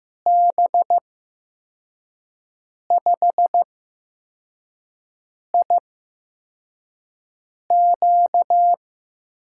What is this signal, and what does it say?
Morse code, heard as a keyed tone, about 15 wpm, spaced as for 4 wpm, 700 Hz. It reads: B5IQ
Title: USA Call Signs - 6 Characters - 8x Character Spacing - 15wpm